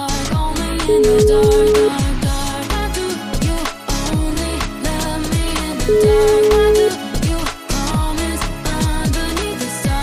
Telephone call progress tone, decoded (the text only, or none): ringback tone